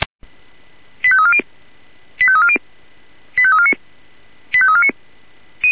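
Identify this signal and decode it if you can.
Five-tone selective call (ZVEI), heard as quick runs of five tones, tones of about 70 ms. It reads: 85340 85340 85349 85349
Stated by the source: ZVEI five-tone selective call received on SDR.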